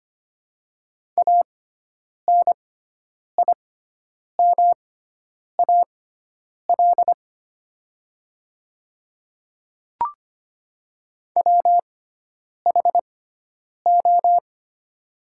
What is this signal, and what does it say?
Morse code, heard as a keyed tone, about 25 wpm, spaced as for 9 wpm, 700 Hz. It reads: ANIMAL WHO
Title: 6x Character Spacing - Top 300 Words - 25wpm